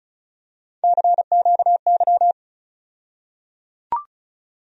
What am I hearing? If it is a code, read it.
Morse code, heard as a keyed tone, about 36 wpm, 700 Hz. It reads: CQY
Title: USA Call Sign Postfixes - 3 Letters 35wpm